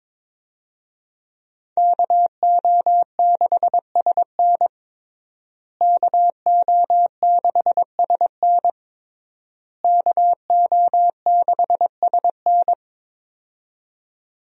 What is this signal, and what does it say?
Morse code, heard as a keyed tone, about 22 wpm, 700 Hz. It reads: KO6SN KO6SN KO6SN
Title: USA Call Signs - 5 Characters - Repeated 3x - 22wpm